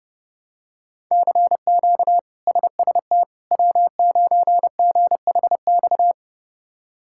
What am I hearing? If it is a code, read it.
Morse code, heard as a keyed tone, about 32 wpm, 700 Hz. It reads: CQ SST W9GHX
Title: K1USN SST QSO Practice - 30wpm